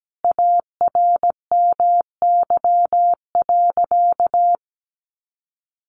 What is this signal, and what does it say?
Morse code, heard as a keyed tone, about 17 wpm, 700 Hz. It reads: ARMY.
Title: Sentences from Top 1200 Words 17wpm